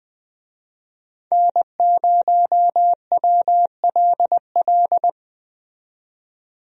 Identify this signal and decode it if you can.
Morse code, heard as a keyed tone, about 20 wpm, 700 Hz. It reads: N0WLL
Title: USA Call Signs - 5 Characters - 20wpm